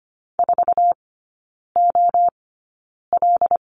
Morse code, heard as a keyed tone, about 25 wpm, spaced as for 9 wpm, 700 Hz. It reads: 4OL
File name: course / 25:10wpm / 093 Lesson 16e - Letter c - Three Characters 25wpm